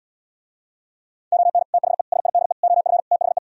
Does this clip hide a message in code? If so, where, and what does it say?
Yes: Morse code from 1.3 to 3.4 s, heard as a keyed tone, about 77 wpm, 690 Hz. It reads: ON THE SIDE OF THE